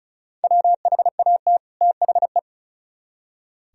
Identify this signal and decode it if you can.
Morse code, heard as a keyed tone, about 45 wpm, 700 Hz. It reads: WHAT THE